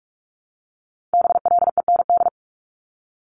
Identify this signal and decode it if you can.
Morse code, heard as a keyed tone, about 46 wpm, 700 Hz. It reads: BLEND
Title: Multi-letter Phonemes 45wpm